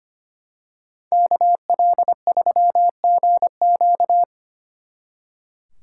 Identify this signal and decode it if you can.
Morse code, heard as a keyed tone, about 29 wpm, 700 Hz. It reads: KL3GQ
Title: Speed Racing - USA 2x2 Call Signs 25wpm